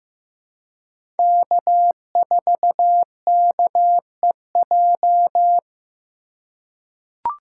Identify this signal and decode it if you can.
Morse code, heard as a keyed tone, about 15 wpm, 700 Hz. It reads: K4KEJ
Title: USA Call Signs - 5 Characters - 2x Character Spacing - 15wpm